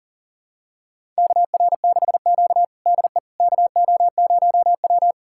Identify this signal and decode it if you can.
Morse code, heard as a keyed tone, about 40 wpm, 700 Hz. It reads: KR6Q DE KO0W
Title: USA Callsign DE Callsign - 4 Characters - 40wpm